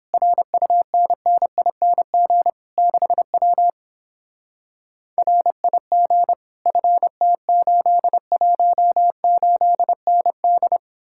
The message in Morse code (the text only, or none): RUNNING 6W RIG FT818ND